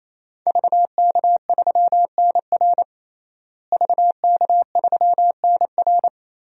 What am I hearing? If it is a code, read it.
Morse code, heard as a keyed tone, about 28 wpm, 700 Hz. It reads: VK3NR VK3NR
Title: DXCC-100 Call Signs - 5 Characters - Repeated 3x - 28wpm